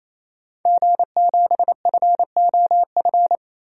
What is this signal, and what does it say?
Morse code, heard as a keyed tone, about 28 wpm, 700 Hz. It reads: G7FOF
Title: DXCC-100 Call Signs - 5 Characters - 28wpm